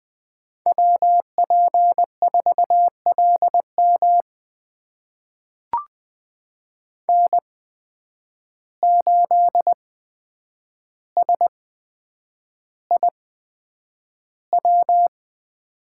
Morse code, heard as a keyed tone, about 21 wpm, spaced as for 9 wpm, 700 Hz. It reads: WP4LM N8SIW
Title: USA Call Signs - 5 Characters - 8x Character Spacing - 20wpm